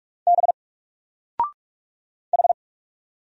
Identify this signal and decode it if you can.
Morse code, heard as a keyed tone, about 46 wpm, 700 Hz. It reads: B H
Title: Single Character (Letter:Number:Punctuation:Prosign) 45wpm